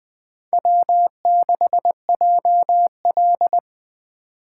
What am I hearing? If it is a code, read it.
Morse code, heard as a keyed tone, about 20 wpm, 700 Hz. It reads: W6JL
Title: Speed Racing - USA 1x2 Call Signs 17wpm